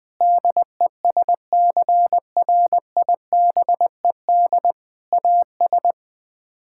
Morse code, heard as a keyed tone, about 20 wpm, 700 Hz. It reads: DESCRIBED AS
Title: Sets of 6 Words - 20wpm